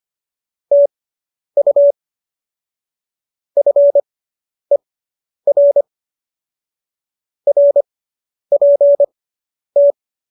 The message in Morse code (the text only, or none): TU FER RPT